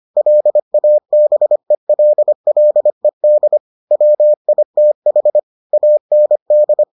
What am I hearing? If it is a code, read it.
Morse code, heard as a keyed tone, about 25 wpm, 600 Hz. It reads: LABELLED WITH AND